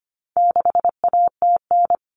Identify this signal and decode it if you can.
Morse code, heard as a keyed tone, about 25 wpm, 700 Hz. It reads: 6ATN
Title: All Cut Numbers - 4-Digit 25wpm